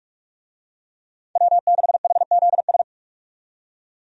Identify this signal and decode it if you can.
Morse code, heard as a keyed tone, about 46 wpm, 700 Hz. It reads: W6HZS
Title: USA Call Signs - 5 Characters - 45wpm